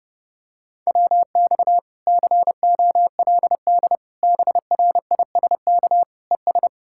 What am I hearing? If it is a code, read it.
Morse code, heard as a keyed tone, about 30 wpm, 700 Hz. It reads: WX COLD BRISK ES